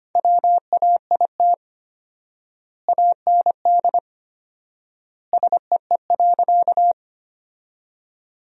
Morse code, heard as a keyed tone, about 25 wpm, 700 Hz. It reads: WAIT AND SEE.
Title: 4x Word-Spacing - Sentences from Top 500 Words - 25wpm